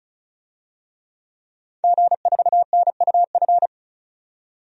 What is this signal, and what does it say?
Morse code, heard as a keyed tone, about 36 wpm, 700 Hz. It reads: G4NUF
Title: DXCC-100 Call Signs - 5 Characters - 35wpm